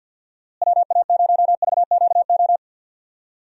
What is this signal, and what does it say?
Morse code, heard as a keyed tone, about 50 wpm, 700 Hz. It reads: WA0VQO
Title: USA Call Signs - 6 Characters - Repeated 3x - 50wpm